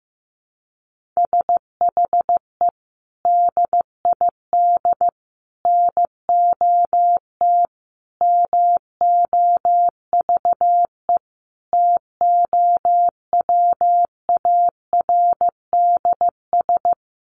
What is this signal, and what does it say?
Morse code, heard as a keyed tone, about 15 wpm, 700 Hz. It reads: SHE DID NOT MOVE TOWARDS
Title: Sentences from Top 1500 Words 15wpm